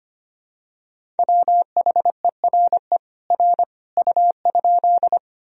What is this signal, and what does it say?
Morse code, heard as a keyed tone, about 25 wpm, 700 Hz. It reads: WHERE R U?
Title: Sentences from Top 200 Words and Abbreviations 25wpm